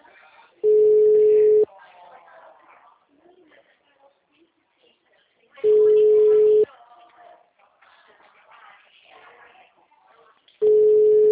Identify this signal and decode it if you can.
Ringback tone (Europe, 425 Hz), with speech.
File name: out- -2009- - - .3704